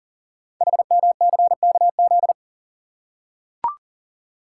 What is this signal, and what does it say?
Morse code, heard as a keyed tone, about 40 wpm, 700 Hz. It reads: HMCKZ
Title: Code Groups - 5 Letters - 40wpm